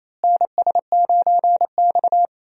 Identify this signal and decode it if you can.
Morse code, heard as a keyed tone, about 28 wpm, 700 Hz. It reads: NS9X
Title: USA Call Signs - 4 Characters - Repeated 2x - 28wpm